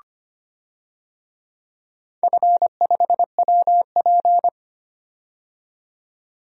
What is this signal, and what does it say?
Morse code, heard as a keyed tone, about 25 wpm, 690 Hz. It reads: F5WP
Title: DXCC-100 Call Signs - 4 Characters - 25wpm